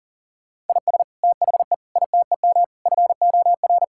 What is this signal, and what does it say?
Morse code, heard as a keyed tone, about 40 wpm, 700 Hz. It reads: IS THE ITEM FOR
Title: Sentences from Top 1000 Words - Review Part 6 - 40wpm